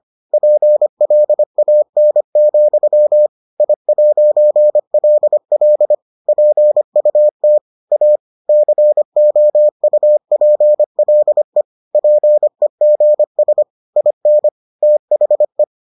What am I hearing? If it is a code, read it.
Morse code, heard as a keyed tone, about 25 wpm, 600 Hz. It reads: PLAN, I'LL PUT A COUPLE PEGS IN THE